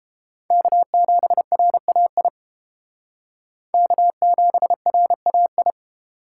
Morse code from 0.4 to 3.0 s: K7RAI

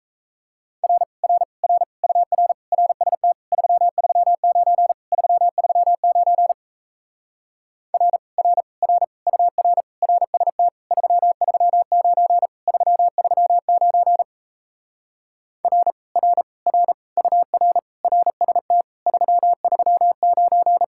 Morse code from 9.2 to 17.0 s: UR RST 339 339 R R R

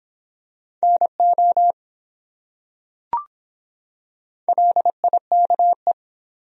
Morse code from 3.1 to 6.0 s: LIKE

